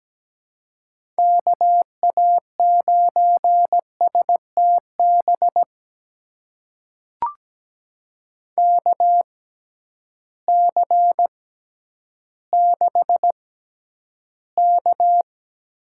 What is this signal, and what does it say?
Morse code, heard as a keyed tone, about 17 wpm, spaced as for 10 wpm, 700 Hz. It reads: KA9STB KC6K
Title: USA Call Signs - 6 Characters - 6x Character Spacing - 17wpm